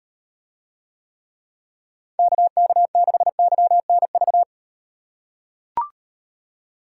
Morse code, heard as a keyed tone, about 38 wpm, 700 Hz. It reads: KK6YNV